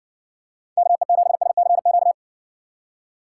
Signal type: Morse code

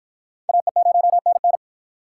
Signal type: Morse code

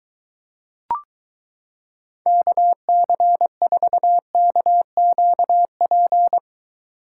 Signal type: Morse code

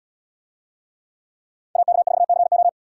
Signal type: Morse code

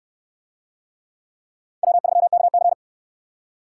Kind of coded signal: Morse code

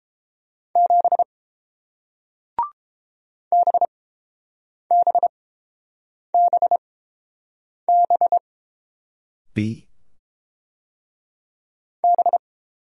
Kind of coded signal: Morse code

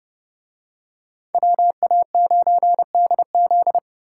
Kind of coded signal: Morse code

